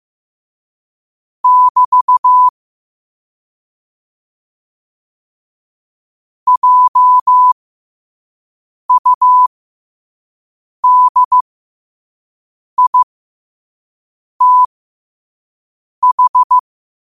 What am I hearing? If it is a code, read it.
Morse code, heard as a keyed tone, about 15 wpm, spaced as for 5 wpm, 1000 Hz. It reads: <BT> JUDITH
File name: qotdfiles / 5wpm